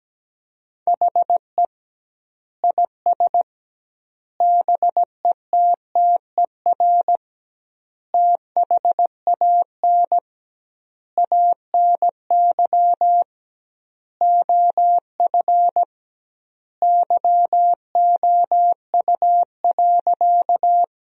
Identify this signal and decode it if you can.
Morse code, heard as a keyed tone, about 17 wpm, 700 Hz. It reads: HE IS BETTER THAN ANY OF YOU.
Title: 2x Word-Spacing - Sentences from Top 400 Words - 17wpm